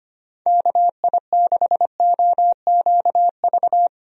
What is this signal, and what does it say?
Morse code, heard as a keyed tone, about 25 wpm, 700 Hz. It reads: KI6OQV
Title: USA Call Signs - 6 Characters - Repeated 2x - 25wpm